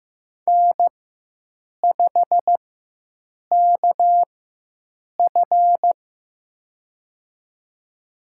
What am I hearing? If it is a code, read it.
Morse code, heard as a keyed tone, about 15 wpm, spaced as for 7 wpm, 700 Hz. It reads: N5KF
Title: USA Call Signs - 4x Character Spacing - 15wpm